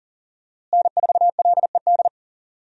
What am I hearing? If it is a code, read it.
Morse code, heard as a keyed tone, about 40 wpm, 700 Hz. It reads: N4LED
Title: USA Call Signs - 5 Characters - Repeated 2x - 40wpm